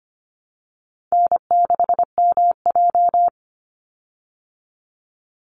Morse code, heard as a keyed tone, about 25 wpm, 700 Hz. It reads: N6MJ
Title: Easy Call Signs Repeated 3x - 25wpm